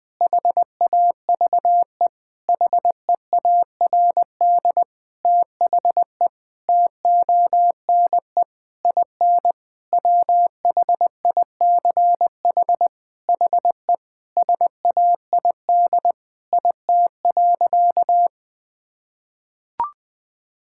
Morse code, heard as a keyed tone, about 20 wpm, 700 Hz. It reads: HAVE HEARD THE TONE IN WHICH HE SAID IT.
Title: Sentences from Top 1600 Words 20wpm